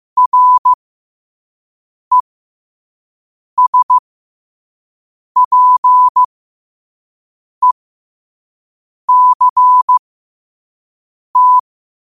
Morse code, heard as a keyed tone, about 15 wpm, spaced as for 5 wpm, 1000 Hz. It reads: RESPECT
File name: qotdfiles / 5wpm